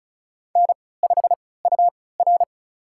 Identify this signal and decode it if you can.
Morse code, heard as a keyed tone, about 36 wpm, spaced as for 20 wpm, 700 Hz. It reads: N5UR
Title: USA Call Signs - 4 Characters - 3x Character Spacing - 35wpm